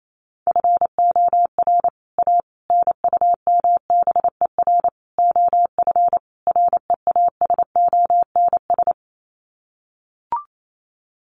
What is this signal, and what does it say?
Morse code, heard as a keyed tone, about 28 wpm, 700 Hz. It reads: FOR A NUMBER OF REASONS
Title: Sets of 5 Words Encore2 28wpm